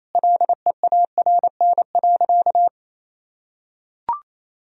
Morse code, heard as a keyed tone, about 28 wpm, 690 Hz. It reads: LEARN.
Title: Sentences from Top 1000 Words - Review Part 3 - 28wpm